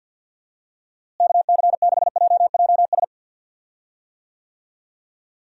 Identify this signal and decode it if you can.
Morse code, heard as a keyed tone, about 47 wpm, 700 Hz. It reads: KC6JJS
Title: USA Call Signs - 6 Characters - Repeated 2x - 50wpm